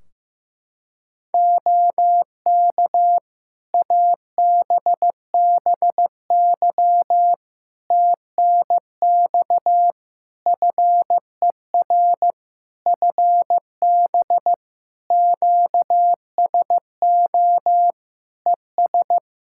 Morse code, heard as a keyed tone, about 15 wpm, 700 Hz. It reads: OK ABBY TNX FER FB QSO ES
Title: Practice QSO - Exchange 4 - Ending - 15wpm